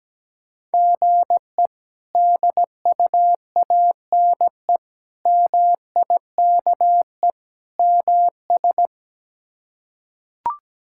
Morse code, heard as a keyed tone, about 17 wpm, 700 Hz. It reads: GE DUANE MIKE MS
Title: K1USN SST QSO Practice - 17wpm